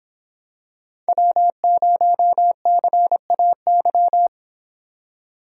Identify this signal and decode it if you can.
Morse code, heard as a keyed tone, about 23 wpm, 700 Hz. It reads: W0CAY